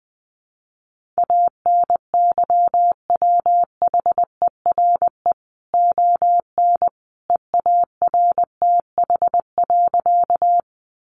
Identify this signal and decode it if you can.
Morse code, heard as a keyed tone, about 20 wpm, 700 Hz. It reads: ANYWHERE ON EARTH.
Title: Sentences from Top 1400 Words 20wpm